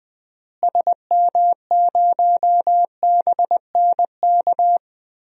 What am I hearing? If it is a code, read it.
Morse code, heard as a keyed tone, about 20 wpm, 700 Hz. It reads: SM0BNK
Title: DXCC-100 Call Signs - 6 Characters - Repeated 3x - 20wpm